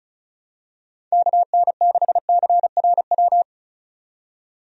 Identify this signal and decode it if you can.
Morse code, heard as a keyed tone, about 36 wpm, 700 Hz. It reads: KN6CRW
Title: USA Call Signs - 6 Characters - 35wpm